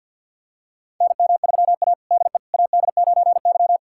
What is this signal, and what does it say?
Morse code, heard as a keyed tone, about 48 wpm, 700 Hz. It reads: NM3A DE AD9Y
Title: USA Callsign DE Callsign - 4 Characters - 50wpm